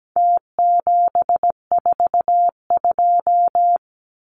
Morse code, heard as a keyed tone, about 17 wpm, 700 Hz. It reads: T742
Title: Numbers - Contest Style - 4-Digit 17wpm